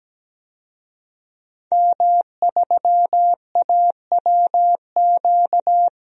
Morse code, heard as a keyed tone, about 17 wpm, 700 Hz. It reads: M3AWQ
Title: DXCC-100 Call Signs - 5 Characters - 17wpm